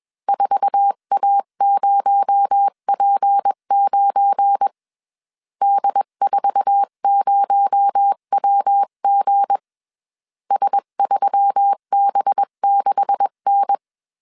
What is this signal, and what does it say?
Morse code, heard as a keyed tone, about 21 wpm, 800 Hz. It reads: 4A0P9 D40WG S3B6N
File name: CW / DET / 1 9 1500C S B 20wpm 2313